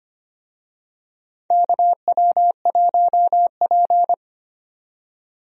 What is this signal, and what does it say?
Morse code, heard as a keyed tone, about 25 wpm, 700 Hz. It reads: KW1P
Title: USA Call Signs - 4 Characters - 25wpm